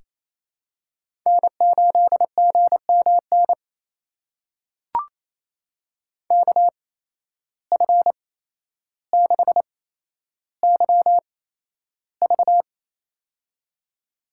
Morse code, heard as a keyed tone, about 28 wpm, spaced as for 12 wpm, 700 Hz. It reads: N8GMN KF6YV
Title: USA Call Signs - 5 Characters - 8x Character Spacing - 28wpm